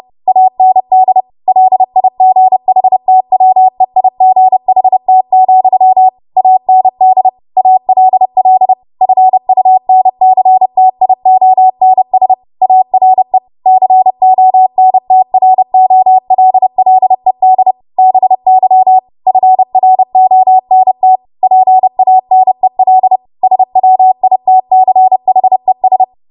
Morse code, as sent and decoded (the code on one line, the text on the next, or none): .- -. -.. / .-.. .. --. .... - .-- . .. --. .... - --..-- / .- -. -.. / .- .-.. .-.. / ..-. ..- -. -.-. - .. --- -. ... / .- .-. / -.-. --- -. - .-. --- .-.. .-.. -.. / -... -.-- / ..-. .-. --- -. - / .--. .- -. .-.. / ... .-- .. - -.-. .... ...
AND LIGHTWEIGHT, AND ALL FUNCTIONS ARE CONTROLLED BY FRONT PANEL SWITCHES